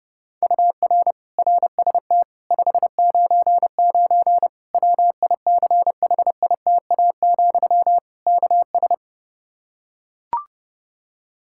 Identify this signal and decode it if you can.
Morse code, heard as a keyed tone, about 32 wpm, 700 Hz. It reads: UR RST 599 WICHITA, KS
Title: Practice QSO - Exchange 1 - Beginning - 30wpm